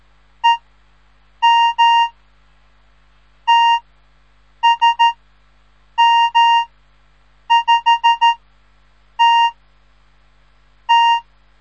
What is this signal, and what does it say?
Morse code, heard as a keyed tone, about 13 wpm, 930 Hz. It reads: E M T S M 5 T T
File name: radiomorningstar / morse / les 3